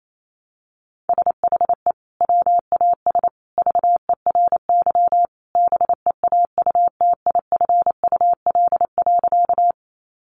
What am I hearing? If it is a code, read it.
Morse code, heard as a keyed tone, about 28 wpm, 700 Hz. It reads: SHE WAS VERY BEAUTIFUL.
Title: Sentences from Top 900 Words 28wpm